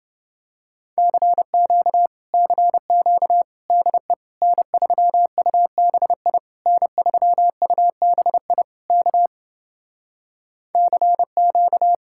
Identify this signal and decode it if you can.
Morse code, heard as a keyed tone, about 29 wpm, 700 Hz. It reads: CQ CQ DE N3UBI N3UBI K CQ